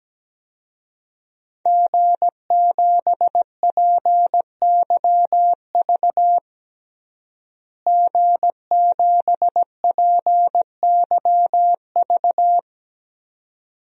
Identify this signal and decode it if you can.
Morse code, heard as a keyed tone, about 17 wpm, 700 Hz. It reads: G7PYV G7PYV
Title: DXCC-100 Call Signs - 5 Characters - Repeated 2x - 17wpm